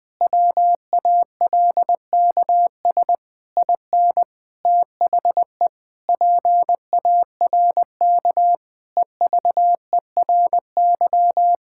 Morse code, heard as a keyed tone, about 21 wpm, 700 Hz. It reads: WALKS IN THE PARK EVERY
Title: Sentences with 2-Word Phrases - 20wpm